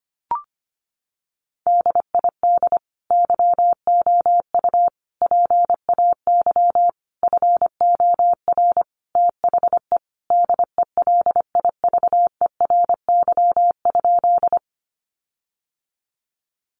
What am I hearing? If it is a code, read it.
Morse code, heard as a keyed tone, about 25 wpm, 700 Hz. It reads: DID YOU PAY FOR THE DELIVERY?
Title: Sentences from Top 1500 Words 25wpm